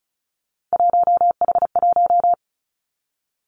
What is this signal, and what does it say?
Morse code, heard as a keyed tone, about 36 wpm, 700 Hz. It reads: 1H1
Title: Three Challenging Letter-Number Combinations 35wpm